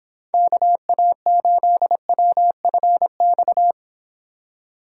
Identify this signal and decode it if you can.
Morse code, heard as a keyed tone, about 30 wpm, 700 Hz. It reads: KA8WFX